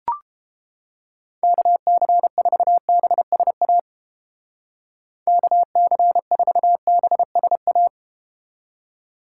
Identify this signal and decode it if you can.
Morse code, heard as a keyed tone, about 30 wpm, 700 Hz. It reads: KC4BSA KC4BSA